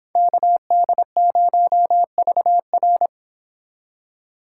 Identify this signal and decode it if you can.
Morse code, heard as a keyed tone, about 26 wpm, 700 Hz. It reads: KD0VR
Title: Speed Racing - USA 2x2 Call Signs 22wpm